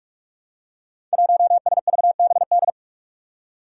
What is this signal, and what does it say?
Morse code, heard as a keyed tone, about 44 wpm, 700 Hz. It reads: 1SVBD
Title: Five Challenging Letter-Number Combinations 45wpm